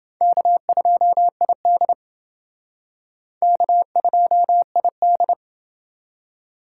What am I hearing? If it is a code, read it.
Morse code, heard as a keyed tone, about 29 wpm, 690 Hz. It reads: K2ID K2ID